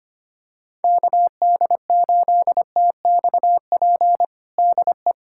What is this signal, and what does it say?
Morse code, heard as a keyed tone, about 25 wpm, 700 Hz. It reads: KD8TXP DE
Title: USA Callsign DE Callsign - 6 Characters - 25wpm